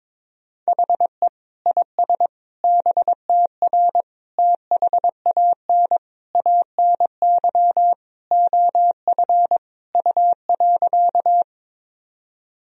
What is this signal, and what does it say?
Morse code, heard as a keyed tone, about 22 wpm, 700 Hz. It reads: HE IS BTR THAN ANY OF U.
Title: Sentences from Top 400 Words and Abbreviations 22wpm